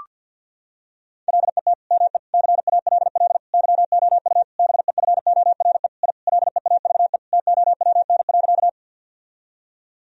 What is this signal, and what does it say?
Morse code, heard as a keyed tone, about 52 wpm, 700 Hz. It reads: LET ME CALL YOU BEFORE I LEAVE TOWN.
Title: Sentences from Top 1000 Words - Review Part 2 - 50wpm